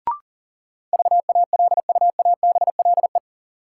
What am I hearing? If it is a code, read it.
Morse code, heard as a keyed tone, about 39 wpm, 700 Hz. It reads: VALUABLE
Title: Sentences from Top 1100 Words - New Word Drill - 40wpm